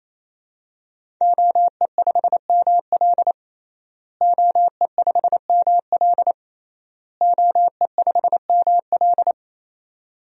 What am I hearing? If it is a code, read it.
Morse code, heard as a keyed tone, about 28 wpm, 700 Hz. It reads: OE5ML OE5ML OE5ML
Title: DXCC-100 Call Signs - 5 Characters - Repeated 3x - 28wpm